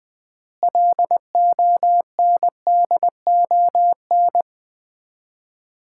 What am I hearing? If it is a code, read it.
Morse code, heard as a keyed tone, about 20 wpm, 700 Hz. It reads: LONDON
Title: Common QSO Phrases - 3x Word-Spacing - 20wpm